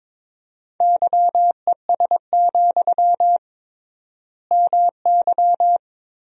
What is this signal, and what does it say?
Morse code, heard as a keyed tone, about 21 wpm, 700 Hz. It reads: YES, MY
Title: 3x Word-Spacing - Sentences from Top 500 Words - 22wpm